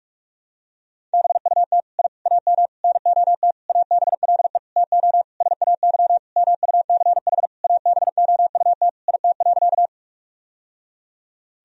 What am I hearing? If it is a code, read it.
Morse code, heard as a keyed tone, about 46 wpm, 700 Hz. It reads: BUT I AM NOT ABLE TO SAY MUCH ABOUT IT.
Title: Sentences from Top 1000 Words - Review Part 2 - 45wpm